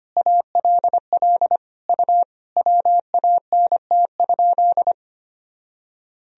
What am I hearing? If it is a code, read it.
Morse code, heard as a keyed tone, about 25 wpm, 700 Hz. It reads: ALL U WANT?
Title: Sentences from Top 200 Words and Abbreviations 25wpm